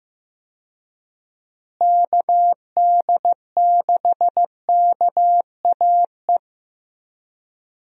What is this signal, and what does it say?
Morse code, heard as a keyed tone, about 15 wpm, 700 Hz. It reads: KD6KAE
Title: USA Call Signs - 6 Characters - Repeated 2x - 15wpm